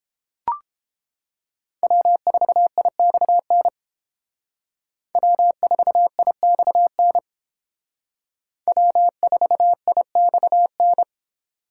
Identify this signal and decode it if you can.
Morse code, heard as a keyed tone, about 31 wpm, 700 Hz. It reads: W4IXN W4IXN W4IXN